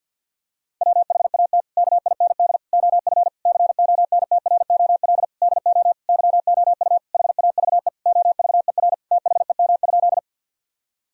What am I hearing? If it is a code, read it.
Morse code, heard as a keyed tone, about 50 wpm, 700 Hz. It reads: WHAT KIND OF CONTROL DO YOU HAVE OVER THEM?
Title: Sentences from Top 1000 Words - Review Part 3 - 50wpm